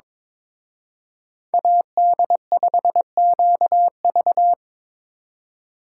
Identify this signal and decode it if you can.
Morse code, heard as a keyed tone, about 25 wpm, 700 Hz. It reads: AD5QV
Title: Speed Racing - USA 2x2 Call Signs 22wpm